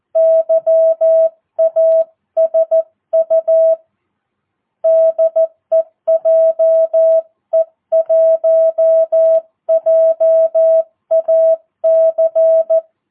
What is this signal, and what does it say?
Morse code, heard as a keyed tone, about 14 wpm, 650 Hz. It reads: YASU DEJE1JAC
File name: jh8cbh / 2022cw